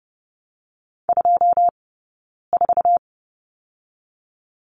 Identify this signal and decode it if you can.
Morse code, heard as a keyed tone, about 30 wpm, 700 Hz. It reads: 2 4
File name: Lesson 18d - Letter m - Two Characters 30wpm